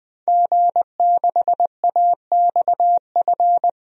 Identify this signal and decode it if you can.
Morse code, heard as a keyed tone, about 20 wpm, 700 Hz. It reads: G6AXF
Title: DXCC-100 Call Signs - 5 Characters - 20wpm